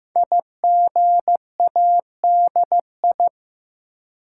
Morse code, heard as a keyed tone, about 15 wpm, 700 Hz. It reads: IGADI
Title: Code Groups - 5 Letters - 15wpm